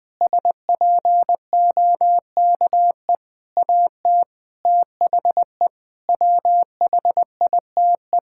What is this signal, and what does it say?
Morse code, heard as a keyed tone, about 21 wpm, 700 Hz. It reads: SPOKE AT THE WHITE
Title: Sentences with 4-Word Phrases - 20wpm